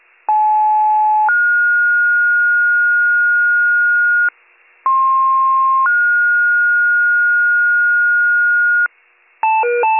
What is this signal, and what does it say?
Two-tone page, heard as a one-second tone, then a three-second tone, about 860 then 1430 Hz, about 1060 then 1430 Hz. It is heard from 0.3 to 8.9 s.